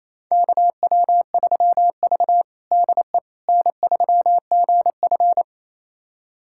Morse code, heard as a keyed tone, about 28 wpm, 700 Hz. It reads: KW3V DE N3GF
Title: USA Callsign DE Callsign - 4 Characters - 28wpm